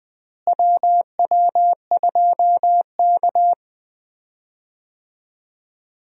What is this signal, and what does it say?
Morse code, heard as a keyed tone, about 20 wpm, 700 Hz. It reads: WW2K
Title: USA Call Signs - 4 Characters - Repeated 3x - 20wpm